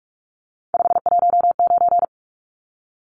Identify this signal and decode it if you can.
Morse code, heard as a keyed tone, about 44 wpm, 700 Hz. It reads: 519
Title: Numbers - 3-Digit 45wpm